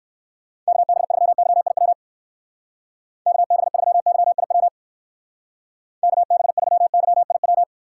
Morse code, heard as a keyed tone, about 63 wpm, 700 Hz. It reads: KB3YIW KB3YIW KB3YIW